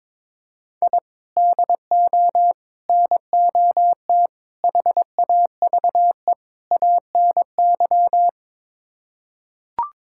Morse code, heard as a keyed tone, about 22 wpm, 700 Hz. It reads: I DO NOT HAVE ANY
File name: practice / 22wpm / Build A Phrase - 5-Word Phrases - 22wpm